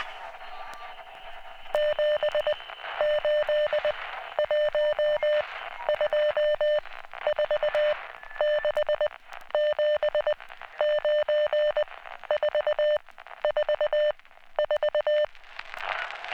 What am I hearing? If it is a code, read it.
Morse code, heard as a keyed tone, about 20 wpm, 600 Hz, with radio noise. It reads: 7 8 1 2 4 6 7 9 4 4 4